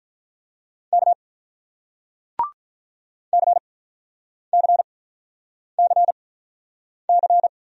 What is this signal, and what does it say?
Morse code, heard as a keyed tone, about 47 wpm, spaced as for 9 wpm, 700 Hz. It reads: K CCCC